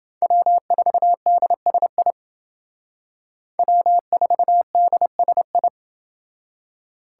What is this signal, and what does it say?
Morse code, heard as a keyed tone, about 28 wpm, 700 Hz. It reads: W4DSI W4DSI